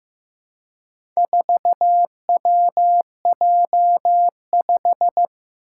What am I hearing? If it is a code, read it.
Morse code, heard as a keyed tone, about 15 wpm, 700 Hz. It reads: 4WJ5
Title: Four Challenging Letter-Number Combinations 15wpm